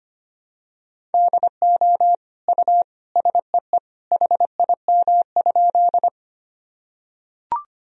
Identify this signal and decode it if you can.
Morse code, heard as a keyed tone, about 25 wpm, 700 Hz. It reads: DO U SEE HIM?
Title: Sentences from Top 100 Words and Abbreviations 25wpm